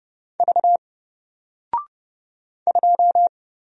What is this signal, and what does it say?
Morse code, heard as a keyed tone, about 31 wpm, 700 Hz. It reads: V 2
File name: Single-Letter-Number-30wpm